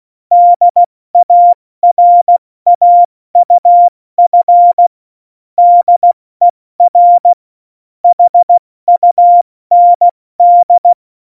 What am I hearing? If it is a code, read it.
Morse code, heard as a keyed tone, about 16 wpm, 700 Hz. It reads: DARAUF DER HUND
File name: Morsekurs / Witze / Witz 05. Juni 2018, 70 F80